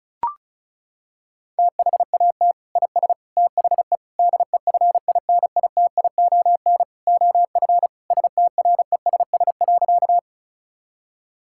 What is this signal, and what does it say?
Morse code, heard as a keyed tone, about 36 wpm, 700 Hz. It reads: THAT IS THE DEFINITION OF STRESS.
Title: Sentences from Top 1000 Words - Review Part 6 - 35wpm